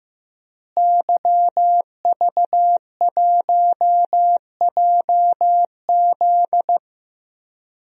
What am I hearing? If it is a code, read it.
Morse code, heard as a keyed tone, about 15 wpm, 700 Hz. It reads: YV1JZ
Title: DXCC-100 Call Signs - 5 Characters - Repeated 3x - 15wpm